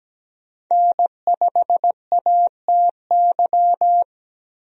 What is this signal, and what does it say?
Morse code, heard as a keyed tone, about 17 wpm, 700 Hz. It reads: N5ATY
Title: USA Call Signs - 5 Characters - Repeated 2x - 17wpm